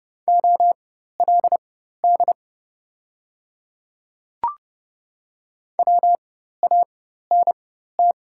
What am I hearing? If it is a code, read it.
Morse code, heard as a keyed tone, about 30 wpm, spaced as for 14 wpm, 690 Hz. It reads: OLD WANT
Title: 4x Character Spacing - Top 500 Words - 30wpm